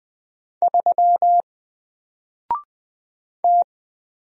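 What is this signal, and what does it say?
Morse code, heard as a keyed tone, about 20 wpm, 700 Hz. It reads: 3 T
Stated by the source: Single letter and number Morse drill at 20 wpm, from N4LCD.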